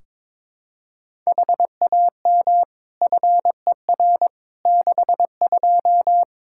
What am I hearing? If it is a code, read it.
Morse code, heard as a keyed tone, about 21 wpm, 700 Hz. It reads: HAM FER 62
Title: K1USN SST QSO Practice 22wpm